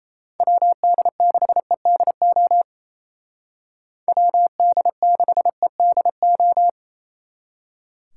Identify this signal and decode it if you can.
Morse code, heard as a keyed tone, about 34 wpm, 700 Hz. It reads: WD6EDO WD6EDO
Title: Speed Racing - USA 2x3 Call Signs 28wpm